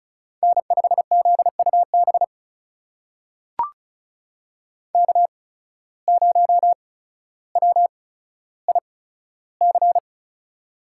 Morse code, heard as a keyed tone, about 36 wpm, spaced as for 15 wpm, 700 Hz. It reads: N5ZUB K0WIC
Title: USA Call Signs - 5 Characters - 8x Character Spacing - 35wpm